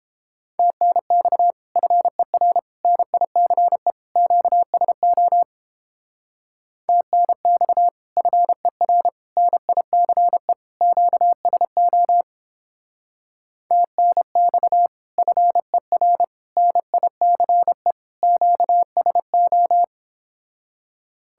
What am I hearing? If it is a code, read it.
Morse code, heard as a keyed tone, about 29 wpm, 700 Hz. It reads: TNX FER NICE QSO TNX FER NICE QSO TNX FER NICE QSO